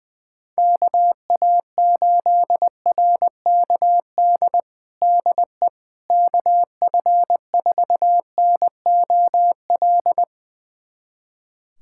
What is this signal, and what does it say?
Morse code, heard as a keyed tone, about 20 wpm, 700 Hz. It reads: KA8RKD DE KF4NOL
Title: USA Callsign DE Callsign - 6 Characters - 20wpm